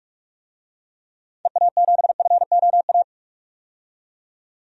Morse code, heard as a keyed tone, about 46 wpm, 700 Hz. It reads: EA7FOA
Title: DXCC-100 Call Signs - 6 Characters - 45wpm